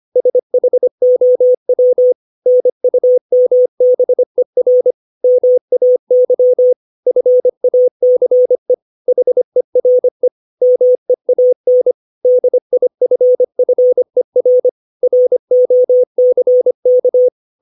Morse code, heard as a keyed tone, about 25 wpm, 500 Hz. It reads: SHOW NUMBER MAY FACE HERE MEAN DIFFER ROCK
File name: 632 words / 632 Words 25WPM